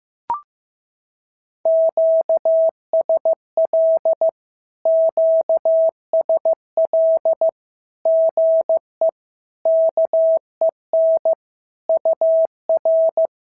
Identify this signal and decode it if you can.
Morse code, heard as a keyed tone, about 15 wpm, 650 Hz. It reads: QSL QSL GE KEN UR
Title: SOTA QSO Practice 15wpm